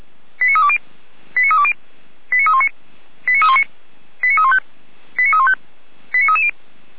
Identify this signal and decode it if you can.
Five-tone selective call (ZVEI), heard as quick runs of five tones, tones of about 80 ms. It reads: 89320 89320 89319 89319 89315 89315 89330